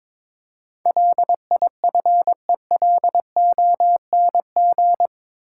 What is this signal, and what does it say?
Morse code, heard as a keyed tone, about 22 wpm, 700 Hz. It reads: LIFELONG
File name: practice / 22wpm / Eight Letter Words - 22wpm